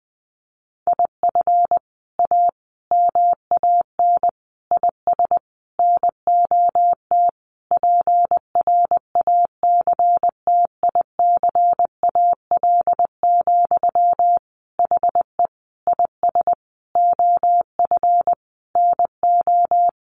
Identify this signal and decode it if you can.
Morse code, heard as a keyed tone, about 20 wpm, 700 Hz. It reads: IF A MAN IS NOT PRACTICAL, HE IS OF NO
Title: Sentences from Top 1400 Words 20wpm